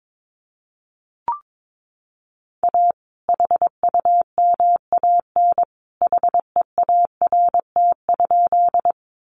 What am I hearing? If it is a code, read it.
Morse code, heard as a keyed tone, about 22 wpm, 700 Hz. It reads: A HUMAN HEART?
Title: Sentences from Top 1200 Words 22wpm